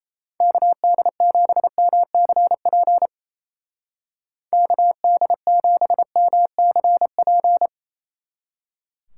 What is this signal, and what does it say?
Morse code, heard as a keyed tone, about 35 wpm, 700 Hz. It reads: KD7MCP KD7MCP
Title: Speed Racing - USA 2x3 Call Signs 28wpm